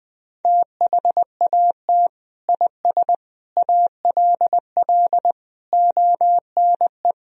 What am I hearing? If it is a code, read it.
Morse code, heard as a keyed tone, about 21 wpm, 700 Hz. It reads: THAT IS ALL ONE
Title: Sentences from Top 200 Words and Abbreviations 20wpm